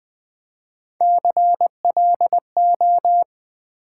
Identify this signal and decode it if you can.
Morse code, heard as a keyed tone, about 20 wpm, 700 Hz. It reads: CLO
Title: Build A Word - 3 Letters At A Time - 6-Letter Words - 20wpm